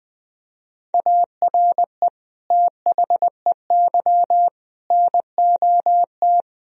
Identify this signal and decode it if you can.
Morse code, heard as a keyed tone, about 21 wpm, 700 Hz. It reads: ARE THEY NOT
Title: Sentences from Top 1000 Words - Review Part 1 - 20wpm